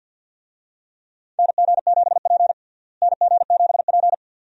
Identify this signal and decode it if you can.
Morse code, heard as a keyed tone, about 52 wpm, 700 Hz. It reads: NG7P NG7P
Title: USA Call Signs - 4 Characters - Repeated 2x - 50wpm